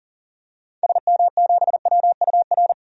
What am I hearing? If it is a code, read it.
Morse code, heard as a keyed tone, about 40 wpm, 700 Hz. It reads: SM7WUR
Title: DXCC-100 Call Signs - 6 Characters - 40wpm